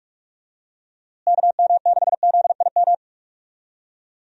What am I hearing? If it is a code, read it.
Morse code, heard as a keyed tone, about 44 wpm, 700 Hz. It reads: KM6ZIM
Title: USA Call Signs - 6 Characters - Repeated 2x - 45wpm